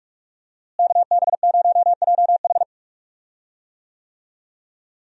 Morse code, heard as a keyed tone, about 43 wpm, 700 Hz. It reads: KB0JH
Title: USA Call Signs - 5 Characters - Repeated 3x - 45wpm